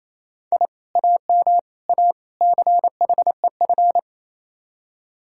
Morse code, heard as a keyed tone, about 36 wpm, 700 Hz. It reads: I AM A CHEF